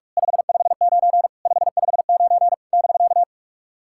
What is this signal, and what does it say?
Morse code, heard as a keyed tone, about 44 wpm, 700 Hz. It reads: 559 559 <BK>
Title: SOTA QSO Practice 45wpm